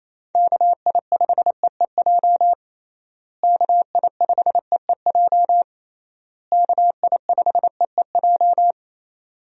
Morse code, heard as a keyed tone, about 27 wpm, 700 Hz. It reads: KI5EEJ KI5EEJ KI5EEJ
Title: USA Call Signs - 6 Characters - Repeated 3x - 28wpm